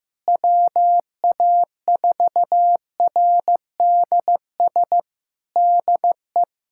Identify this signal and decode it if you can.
Morse code, heard as a keyed tone, about 15 wpm, 700 Hz. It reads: WA4RDS DE
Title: Practice QSO - Exchange 4 - Ending - 15wpm